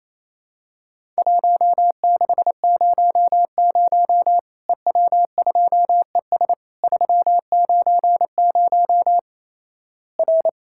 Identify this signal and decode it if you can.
Morse code, heard as a keyed tone, about 28 wpm, 700 Hz. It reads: 1600 EW2ES 390 R
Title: Worked All Europe DX Contest - QTC Exchange - 28wpm